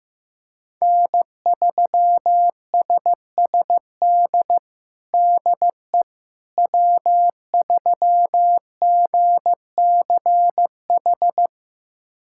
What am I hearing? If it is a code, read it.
Morse code, heard as a keyed tone, about 15 wpm, 700 Hz. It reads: N3SSD DE W3GCH
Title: USA Callsign DE Callsign - 5 Characters - 15wpm